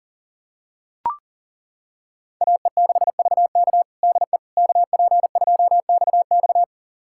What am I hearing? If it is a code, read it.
Morse code, heard as a keyed tone, about 40 wpm, 700 Hz. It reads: AE6VK DE KP2XX
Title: USA Callsign DE Callsign - 5 Characters - 40wpm